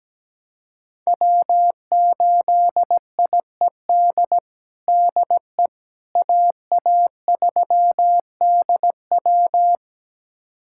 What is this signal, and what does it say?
Morse code, heard as a keyed tone, about 17 wpm, 700 Hz. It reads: W8IED DE AA3DW
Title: USA Callsign DE Callsign - 5 Characters - 17wpm